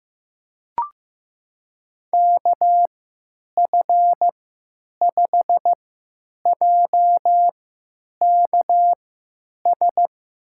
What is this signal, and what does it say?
Morse code, heard as a keyed tone, about 15 wpm, spaced as for 9 wpm, 700 Hz. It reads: KF5JKS
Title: USA Call Signs - 6 Characters - 3x Character Spacing - 15wpm